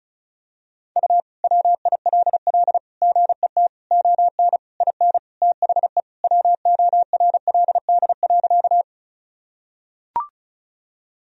Morse code, heard as a keyed tone, about 34 wpm, 700 Hz. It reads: U WILL GET ON IN THE WORLD.
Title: Sentences from Top 200 Words and Abbreviations 35wpm